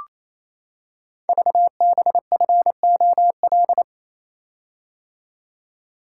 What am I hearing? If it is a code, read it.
Morse code, heard as a keyed tone, about 28 wpm, 700 Hz. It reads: VBFOL
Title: Code Groups - 5 Letters - 28wpm